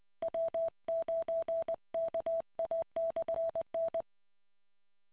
Morse code, heard as a keyed tone, about 20 wpm, 660 Hz. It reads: W9KA/N